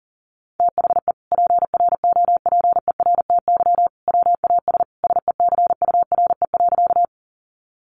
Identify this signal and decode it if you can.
Morse code, heard as a keyed tone, about 37 wpm, 700 Hz. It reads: THE PROPERTY WAS SECURE.
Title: Sentences from Top 1200 Words 40wpm